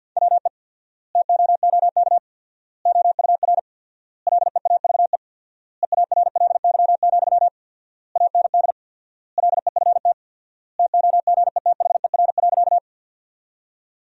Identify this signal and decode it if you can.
Morse code, heard as a keyed tone, about 50 wpm, 700 Hz. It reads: WE TOOK OUR LEAVE EARLY, AND LEFT TOGETHER.
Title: 4x Word-Spacing - Sentences from Top 400 Words - 50wpm